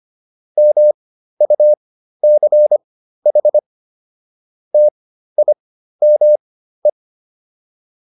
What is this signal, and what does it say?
Morse code, heard as a keyed tone, about 25 wpm, spaced as for 13 wpm, 600 Hz. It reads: MUCH TIME